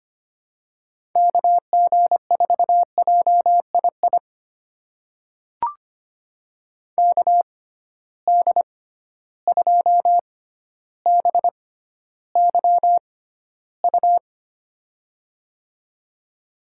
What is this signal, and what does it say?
Morse code, heard as a keyed tone, about 25 wpm, spaced as for 13 wpm, 700 Hz. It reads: KG4JII KD2BYU
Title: USA Call Signs - 6 Characters - 6x Character Spacing - 25wpm